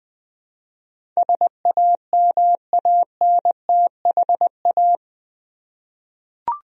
Morse code, heard as a keyed tone, about 21 wpm, 700 Hz. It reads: SAMANTHA
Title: USA Common Names - 20wpm